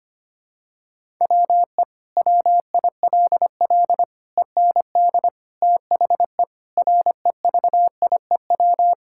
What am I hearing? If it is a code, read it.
Morse code, heard as a keyed tone, about 25 wpm, 700 Hz. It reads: WE WILL END THE REVIEW
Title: Sentences from Top 1000 Words - Review Part 6 - 25wpm